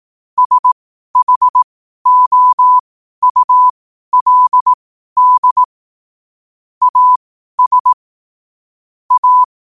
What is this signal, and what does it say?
Morse code, heard as a keyed tone, about 18 wpm, spaced as for 12 wpm, 1000 Hz. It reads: SHOULD AS A
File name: qotdfiles / 12wpm